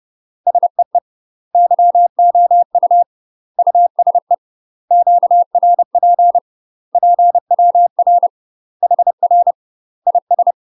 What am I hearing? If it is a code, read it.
Morse code, heard as a keyed tone, about 30 wpm, 700 Hz. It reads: SEE YOU USE QRP PWR HR IS
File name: UK MASTER QSOs TRAINING / 30WPM / UK MASTER QSOs 0030